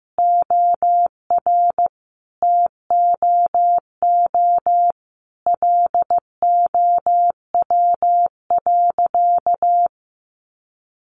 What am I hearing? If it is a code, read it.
Morse code, heard as a keyed tone, about 15 wpm, 700 Hz. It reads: OR TOO LOW.
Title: Sentences from Top 1200 Words 15wpm